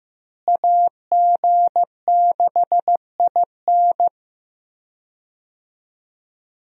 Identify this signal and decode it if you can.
Morse code, heard as a keyed tone, about 15 wpm, 700 Hz. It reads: AG6IN
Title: USA Call Signs - 5 Characters - Repeated 3x - 15wpm